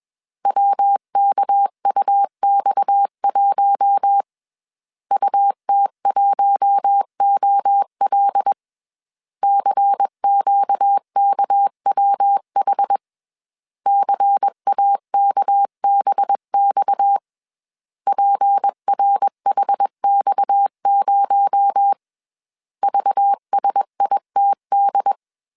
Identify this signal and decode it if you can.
Morse code, heard as a keyed tone, about 21 wpm, 800 Hz. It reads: WKUX1 UT1OL CQKWH CAKBX PRHX0 VSITD
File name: CW / DET / 1 6 1500C S C 20wpm 2116